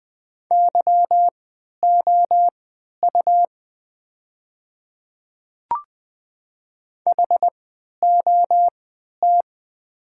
Morse code, heard as a keyed tone, about 20 wpm, spaced as for 11 wpm, 700 Hz. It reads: YOU HOT
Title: 3x Character Spacing - Top 100 Words - 20wpm